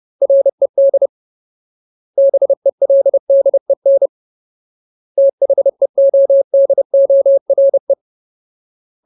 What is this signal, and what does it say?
Morse code, heard as a keyed tone, about 30 wpm, 550 Hz. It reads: RED BELDEN THEODORE
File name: Ultimate Word List / 30WPM Ultimate